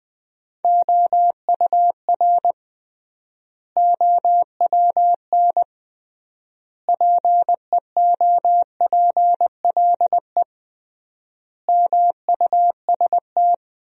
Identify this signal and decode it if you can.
Morse code, heard as a keyed tone, about 20 wpm, 700 Hz. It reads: OUR OWN PEOPLE MUST
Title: 3x Word-Spacing - Sentences from Top 300 Words - 20wpm